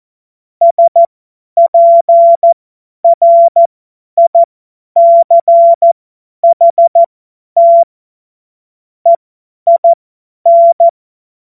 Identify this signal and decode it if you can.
Morse code, heard as a keyed tone, about 14 wpm, spaced as for 10 wpm, 680 Hz. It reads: SPRICHT EIN
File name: Morsekurs / Witze / Witz 19. April 2022 50 F70